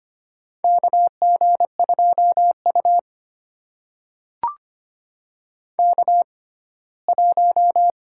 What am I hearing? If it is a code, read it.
Morse code, heard as a keyed tone, about 25 wpm, spaced as for 17 wpm, 700 Hz. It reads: KG2U K1
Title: USA Call Signs - 4 Characters - 6x Character Spacing - 25wpm